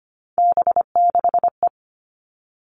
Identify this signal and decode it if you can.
Morse code, heard as a keyed tone, about 25 wpm, 700 Hz. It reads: B6E
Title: All Cut Numbers - 3-Digit 25wpm